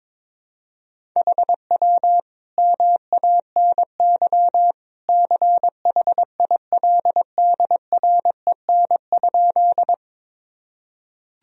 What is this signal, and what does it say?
Morse code, heard as a keyed tone, about 21 wpm, 700 Hz. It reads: HW MANY CHILDREN?
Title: Sentences from Top 300 Words and Abbreviations 22wpm